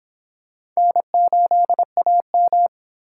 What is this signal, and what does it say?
Morse code, heard as a keyed tone, about 24 wpm, 700 Hz. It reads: N8AM